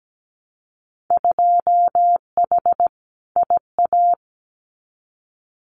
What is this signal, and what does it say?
Morse code, heard as a keyed tone, about 17 wpm, 700 Hz. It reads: 2H IA
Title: Winter Field Day Exchanges Encore 17wpm